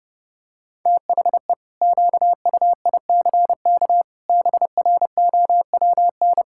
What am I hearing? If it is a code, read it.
Morse code, heard as a keyed tone, about 30 wpm, 700 Hz. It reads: THE QUICK BROWN
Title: Pangram Challenge v2 - 30wpm